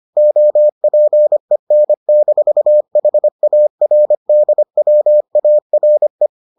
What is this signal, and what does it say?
Morse code, heard as a keyed tone, about 25 wpm, 600 Hz. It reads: OPEN-HARDWARE